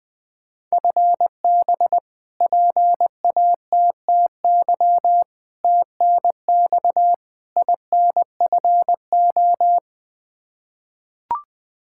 Morse code, heard as a keyed tone, about 21 wpm, 700 Hz. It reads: FB PATTY TNX INFO
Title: K1USN SST QSO Practice 20wpm